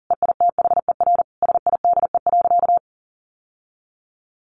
Morse code, heard as a keyed tone, about 40 wpm, 700 Hz. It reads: EITHER SIDE.
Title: Sentences from Top 1200 Words 40wpm